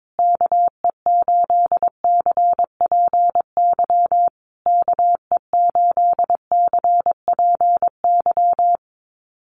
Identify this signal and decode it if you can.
Morse code, heard as a keyed tone, about 22 wpm, 700 Hz. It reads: KE8CPY KE8CPY
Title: USA Call Signs Repeated 3x 22wpm